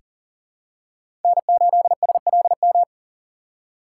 Morse code, heard as a keyed tone, about 40 wpm, 700 Hz. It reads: N8SLM
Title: USA Call Signs - 3x Character Spacing - 40wpm